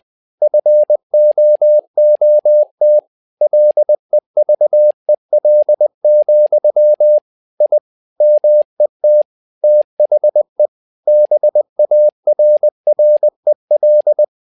Morse code, heard as a keyed tone, about 20 wpm, 600 Hz. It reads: FOOT LEVEL, I MET THE BARREL
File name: tower trouble 20wpm